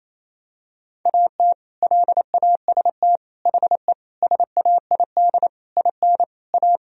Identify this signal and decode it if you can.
Morse code, heard as a keyed tone, about 28 wpm, 700 Hz. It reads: AT LAST HE SAID IN A
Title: Sentences from Top 1000 Words - Review Part 1 - 28wpm